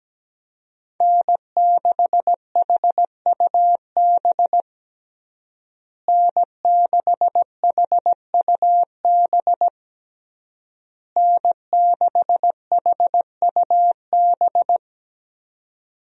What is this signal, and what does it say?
Morse code, heard as a keyed tone, about 17 wpm, 700 Hz. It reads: N6HUB N6HUB N6HUB
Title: USA Call Signs - 5 Characters - Repeated 3x - 17wpm